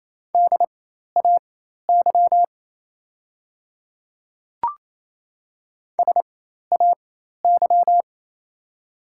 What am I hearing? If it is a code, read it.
Morse code, heard as a keyed tone, about 27 wpm, spaced as for 13 wpm, 690 Hz. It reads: DAY SAY
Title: 4x Character Spacing - Top 200 Words - 28wpm